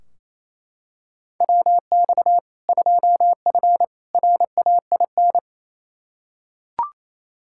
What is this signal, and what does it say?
Morse code, heard as a keyed tone, about 27 wpm, 700 Hz. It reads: WX 2F RAIN
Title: K1USN SST QSO Practice 28wpm